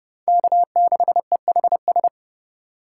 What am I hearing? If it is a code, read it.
Morse code, heard as a keyed tone, about 28 wpm, 700 Hz. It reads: K6EHS